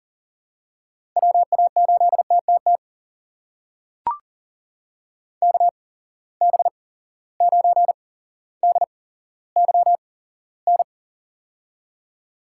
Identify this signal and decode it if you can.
Morse code, heard as a keyed tone, about 40 wpm, spaced as for 17 wpm, 700 Hz. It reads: WA8TTT KB9DYN
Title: USA Call Signs - 6 Characters - 8x Character Spacing - 40wpm